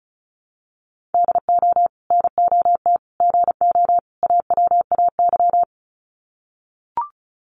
Morse code, heard as a keyed tone, about 34 wpm, 700 Hz. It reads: DO NOT GO AWAY
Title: Sets of 4 Words 35wpm